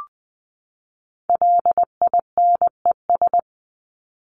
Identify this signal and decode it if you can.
Morse code, heard as a keyed tone, about 20 wpm, 700 Hz. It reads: LINES
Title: ES:S Suffix - Top 100 Words with Base Words 20wpm